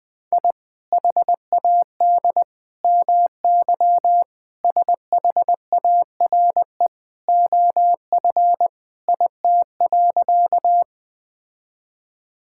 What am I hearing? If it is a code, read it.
Morse code, heard as a keyed tone, about 21 wpm, 700 Hz. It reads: I HAD MY SHARE OF IT.
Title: Sentences from Top 1000 Words - Review Part 6 - 20wpm